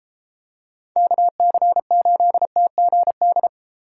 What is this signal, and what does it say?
Morse code, heard as a keyed tone, about 32 wpm, 700 Hz. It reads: KC8TGD